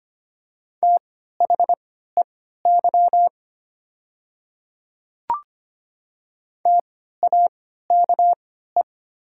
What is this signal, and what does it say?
Morse code, heard as a keyed tone, about 25 wpm, spaced as for 14 wpm, 690 Hz. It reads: THEY TAKE
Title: 3x Character Spacing - Top 200 Words - 25wpm